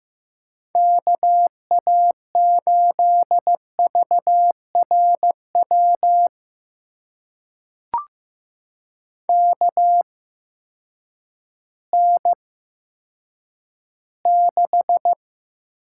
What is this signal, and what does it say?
Morse code, heard as a keyed tone, about 15 wpm, spaced as for 9 wpm, 700 Hz. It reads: KA8VRW KN6
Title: USA Call Signs - 6 Characters - 8x Character Spacing - 15wpm